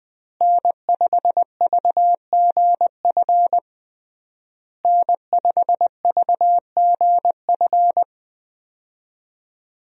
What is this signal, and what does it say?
Morse code, heard as a keyed tone, about 20 wpm, 700 Hz. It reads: N5VGF N5VGF
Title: USA Call Signs - 5 Characters - Repeated 3x - 20wpm